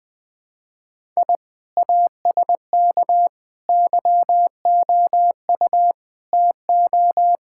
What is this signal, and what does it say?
Morse code, heard as a keyed tone, about 20 wpm, 700 Hz. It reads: I ASK YOU TO
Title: Sentences from Top 1000 Words - Review Part 5 - 20wpm